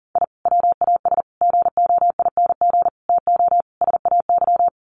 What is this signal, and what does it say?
Morse code, heard as a keyed tone, about 40 wpm, 700 Hz. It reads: I WAS GOING TO SAY
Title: Sets of 5 Words 40wpm